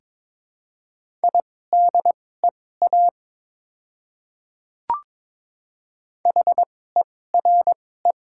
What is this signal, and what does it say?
Morse code, heard as a keyed tone, about 22 wpm, spaced as for 16 wpm, 700 Hz. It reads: IDEA HERE
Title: 2x Character Spacing - Top 300 Words - 22wpm